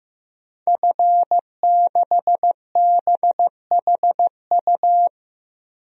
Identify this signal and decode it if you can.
Morse code, heard as a keyed tone, about 15 wpm, 700 Hz. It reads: F6BHU
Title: DXCC-100 Call Signs - 5 Characters - Repeated 2x - 15wpm